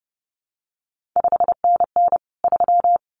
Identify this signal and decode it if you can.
Morse code, heard as a keyed tone, about 30 wpm, 700 Hz. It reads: 5NN 3
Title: CQ WW DX Contest Exchange Encore - 30wpm